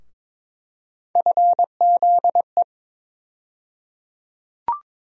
Morse code, heard as a keyed tone, about 22 wpm, 700 Hz. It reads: FZE
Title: USA Call Sign Postfixes - 3 Letters 22wpm